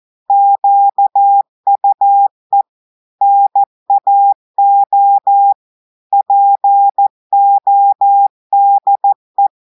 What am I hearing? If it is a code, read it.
Morse code, heard as a keyed tone, about 14 wpm, 800 Hz. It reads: QUE NAO PODE
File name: audios / portugais / portugais texte en clair 1 vitesse 14 mots minutes